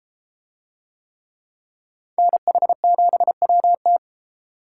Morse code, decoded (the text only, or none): NH7WT